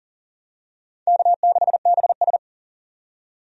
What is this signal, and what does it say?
Morse code, heard as a keyed tone, about 37 wpm, 700 Hz. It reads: K6BS
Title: USA Call Signs - 4 Characters - 40wpm